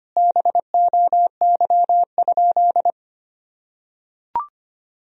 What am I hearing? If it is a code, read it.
Morse code, heard as a keyed tone, about 25 wpm, 700 Hz. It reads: BOY?
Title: Sentences from Top 1000 Words - Review Part 1 - 25wpm